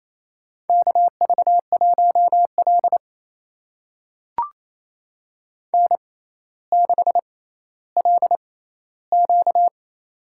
Morse code, heard as a keyed tone, about 28 wpm, spaced as for 14 wpm, 700 Hz. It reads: KV1L N6LQ
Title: USA Call Signs - 4 Characters - 6x Character Spacing - 28wpm